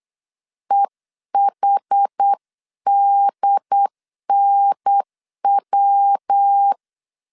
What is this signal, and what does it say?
Morse code, heard as a keyed tone, about 8 wpm, 800 Hz. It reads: EHDNW
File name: CW / DET / 1 3 1500C S B 8wpm 4141